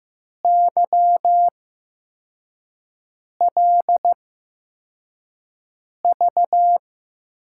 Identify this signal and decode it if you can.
Morse code, heard as a keyed tone, about 15 wpm, spaced as for 4 wpm, 700 Hz. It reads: YLV
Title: USA Call Signs - 8x Character Spacing - 15wpm